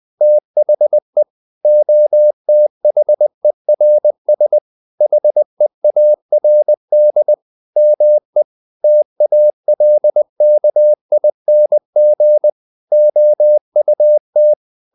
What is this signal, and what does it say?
Morse code, heard as a keyed tone, about 20 wpm, 600 Hz. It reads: THE OTHERS HEARD ME TALKING OUT
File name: the old man and the sea / the old man and the sea part 1 20wpm